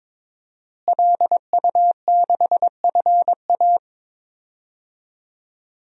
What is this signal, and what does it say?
Morse code, heard as a keyed tone, about 22 wpm, 700 Hz. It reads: LU6FA
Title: DXCC-100 Call Signs - 5 Characters - Repeated 2x - 22wpm